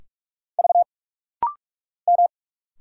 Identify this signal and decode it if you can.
Morse code, heard as a keyed tone, about 46 wpm, 700 Hz. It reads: V M
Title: Single Letters - Rapid-Fire 45wpm